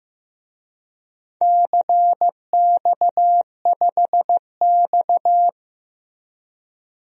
Morse code, heard as a keyed tone, about 16 wpm, 700 Hz. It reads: CX5X
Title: DXCC-100 Call Signs - 4 Characters - Repeated 2x - 15wpm